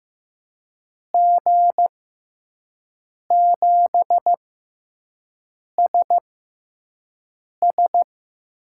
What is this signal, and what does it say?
Morse code, heard as a keyed tone, about 15 wpm, spaced as for 5 wpm, 700 Hz. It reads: G7SS
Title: USA Call Signs - 6x Character Spacing - 15wpm